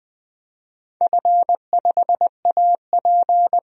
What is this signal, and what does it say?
Morse code, heard as a keyed tone, about 20 wpm, 700 Hz. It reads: F5AP
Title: DXCC-100 Call Signs - 4 Characters - 20wpm